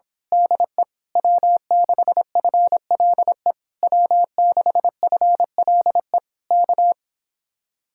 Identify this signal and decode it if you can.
Morse code, heard as a keyed tone, about 26 wpm, 700 Hz. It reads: DE W6FLE W6FLE K
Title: Speed-Racing Common QSO Phrases Encore - 22wpm